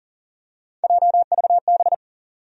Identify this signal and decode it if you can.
Morse code, heard as a keyed tone, about 39 wpm, 700 Hz. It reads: JVB
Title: USA Call Sign Postfixes - 3 Letters 40wpm